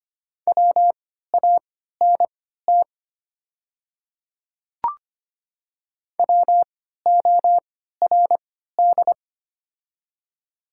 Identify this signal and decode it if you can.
Morse code, heard as a keyed tone, about 25 wpm, spaced as for 14 wpm, 700 Hz. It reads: WANT WORD
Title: 3x Character Spacing - Top 200 Words - 25wpm